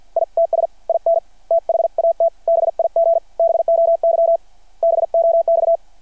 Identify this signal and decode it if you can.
Morse code, heard as a keyed tone, about 53 wpm, spaced as for 47 wpm, 690 Hz. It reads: ITS IN THAT BIG BOY BOX
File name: archived / CRSnet